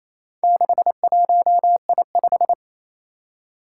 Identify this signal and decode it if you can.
Morse code, heard as a keyed tone, about 27 wpm, 700 Hz. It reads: 61I5
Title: Four Challenging Letter-Number Combinations 28wpm